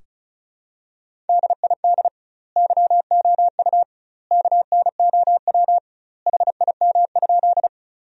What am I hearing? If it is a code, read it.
Morse code, heard as a keyed tone, about 36 wpm, 700 Hz. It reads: DID YOU KNOW HIM?
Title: 2x Word-Spacing - Sentences from Top 100 Words - 35wpm